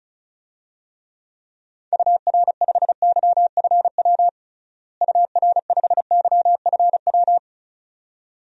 Morse code, heard as a keyed tone, about 34 wpm, 700 Hz. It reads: UR5YFW UR5YFW
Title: DXCC-100 Call Signs - 6 Characters - Repeated 2x - 35wpm